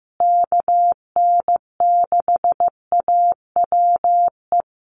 Morse code, heard as a keyed tone, about 15 wpm, 700 Hz. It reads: KN6AWE
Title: USA Call Signs Repeated 3x 15wpm